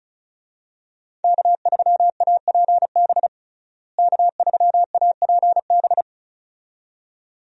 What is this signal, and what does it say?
Morse code, heard as a keyed tone, about 36 wpm, 700 Hz. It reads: K3APB K3APB
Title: USA Call Signs - 5 Characters - Repeated 2x - 35wpm